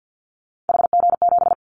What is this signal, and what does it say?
Morse code, heard as a keyed tone, about 50 wpm, 700 Hz. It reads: HG7
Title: Callsign Prefixes for Top 25 DXCC - Repeated 3x - 50wpm